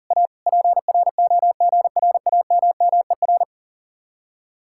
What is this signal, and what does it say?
Morse code, heard as a keyed tone, about 48 wpm, 700 Hz. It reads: A PROGRAMMER
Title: Speed-Racing Common QSO Phrases - 40wpm